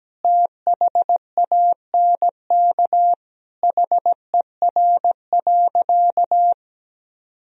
Morse code, heard as a keyed tone, about 17 wpm, 700 Hz. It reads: THANK HER.
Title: Sentences from Top 1000 Words - Review Part 2 - 17wpm